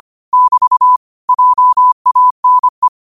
Morse code, heard as a keyed tone, about 25 wpm, 1000 Hz. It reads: <BT> JANE
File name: qotdfiles / 25wpm